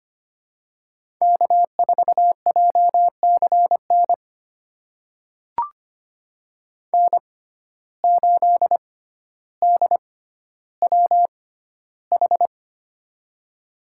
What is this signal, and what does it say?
Morse code, heard as a keyed tone, about 25 wpm, spaced as for 13 wpm, 700 Hz. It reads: K4JCN N8DWH
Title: USA Call Signs - 5 Characters - 6x Character Spacing - 25wpm